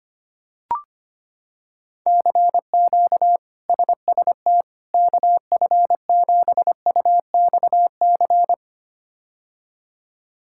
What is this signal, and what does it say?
Morse code, heard as a keyed tone, about 25 wpm, 700 Hz. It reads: CQ SST KF7UXC
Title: K1USN SST QSO Practice - 25wpm